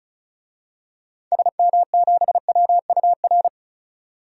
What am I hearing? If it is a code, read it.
Morse code, heard as a keyed tone, about 36 wpm, 700 Hz. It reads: SM7WUR
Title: DXCC-100 Call Signs - 6 Characters - 35wpm